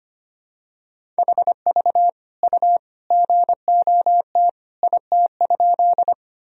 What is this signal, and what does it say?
Morse code, heard as a keyed tone, about 25 wpm, 700 Hz. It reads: HV U GOT IT?
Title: Sentences from Top 300 Words and Abbreviations 25wpm